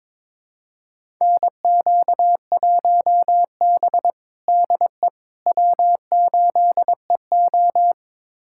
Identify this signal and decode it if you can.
Morse code, heard as a keyed tone, about 22 wpm, 700 Hz. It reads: NQ1B DE W8EO
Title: USA Callsign DE Callsign - 4 Characters - 22wpm